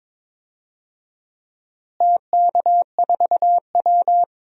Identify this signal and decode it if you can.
Morse code, heard as a keyed tone, about 22 wpm, 700 Hz. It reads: TK4W
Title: DXCC-100 Call Signs - 4 Characters - Repeated 2x - 22wpm